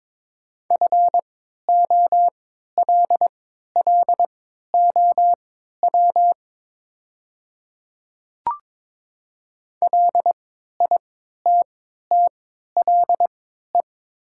Morse code, heard as a keyed tone, about 22 wpm, spaced as for 13 wpm, 700 Hz. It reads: FOLLOW LITTLE
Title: 3x Character Spacing - Top 200 Words - 22wpm